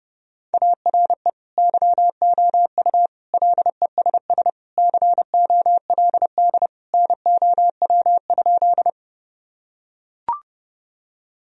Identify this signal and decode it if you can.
Morse code, heard as a keyed tone, about 30 wpm, 700 Hz. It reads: ARE YOU LESS COLD NOW?
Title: Sentences from Top 1000 Words - Review Part 2 - 30wpm